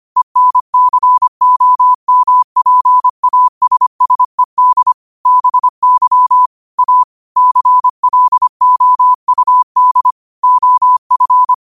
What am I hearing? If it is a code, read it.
Morse code, heard as a keyed tone, about 25 wpm, 1000 Hz. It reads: ENCOMPASSED BY A CLOUD OF
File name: qotdfiles / 25wpm